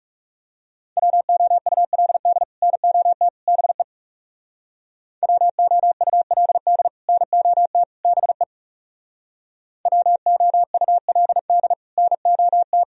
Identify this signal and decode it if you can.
Morse code, heard as a keyed tone, about 39 wpm, 700 Hz. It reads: WOULD NOT BE WOULD NOT BE WOULD NOT